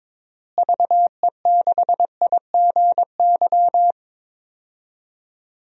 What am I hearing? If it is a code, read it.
Morse code, heard as a keyed tone, about 22 wpm, 700 Hz. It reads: VE6IGY
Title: DXCC-100 Call Signs - 6 Characters - Repeated 2x - 22wpm